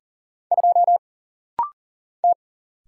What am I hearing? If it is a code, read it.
Morse code, heard as a keyed tone, about 40 wpm, 700 Hz. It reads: 2 T
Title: Single Letter-Number Alt - Rapid-Fire 40wpm